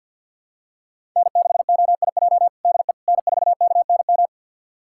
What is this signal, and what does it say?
Morse code, heard as a keyed tone, about 47 wpm, 700 Hz. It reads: N6OIJ DE N4KNM
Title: USA Callsign DE Callsign - 5 Characters - 50wpm